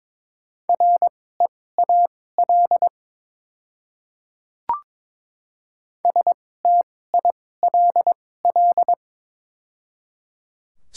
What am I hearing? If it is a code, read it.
Morse code, heard as a keyed tone, about 22 wpm, spaced as for 16 wpm, 700 Hz. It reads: REAL STILL
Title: 2x Character Spacing - Top 300 Words - 22wpm